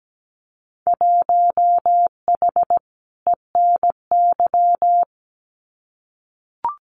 Morse code, heard as a keyed tone, about 17 wpm, 700 Hz. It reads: 1H ENY
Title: Winter Field Day Exchanges 17wpm